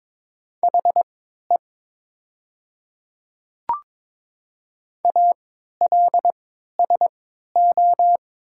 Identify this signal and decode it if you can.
Morse code, heard as a keyed tone, about 22 wpm, spaced as for 12 wpm, 700 Hz. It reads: HE ALSO
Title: 3x Character Spacing - Top 200 Words - 22wpm